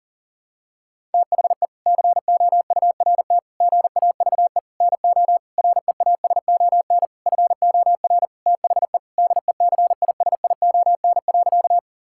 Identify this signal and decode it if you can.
Morse code, heard as a keyed tone, about 40 wpm, 700 Hz. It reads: THE COURT GAVE NO REASON FOR THE DECISION.
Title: Sentences from Top 1600 Words 40wpm